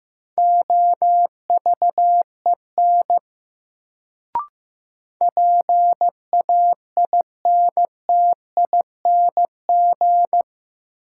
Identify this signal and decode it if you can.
Morse code, heard as a keyed tone, about 15 wpm, 700 Hz. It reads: OVEN PAINTING
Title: Sentences from Top 1200 Words - New Word Drill Alt - 15wpm